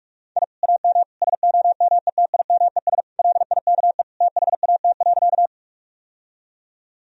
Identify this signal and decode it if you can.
Morse code, heard as a keyed tone, about 41 wpm, 700 Hz. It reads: I AM SOMETIMES LIKE THAT.
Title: Sentences from Top 1000 Words - Review Part 3 - 45wpm